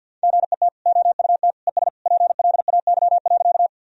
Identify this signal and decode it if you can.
Morse code, heard as a keyed tone, about 50 wpm, 700 Hz. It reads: GET OUT ES PLAY.
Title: Sentences from Top 200 Words and Abbreviations 50wpm